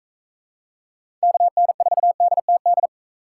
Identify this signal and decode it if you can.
Morse code, heard as a keyed tone, about 42 wpm, 700 Hz. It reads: KN4DTD